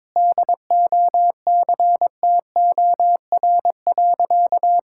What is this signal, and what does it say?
Morse code, heard as a keyed tone, about 22 wpm, 700 Hz. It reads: DOCTOR.
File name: practice / 22wpm / Sentences with 3-Word Phrases - 22wpm